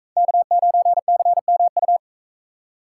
Morse code, heard as a keyed tone, about 43 wpm, 700 Hz. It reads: K9CMU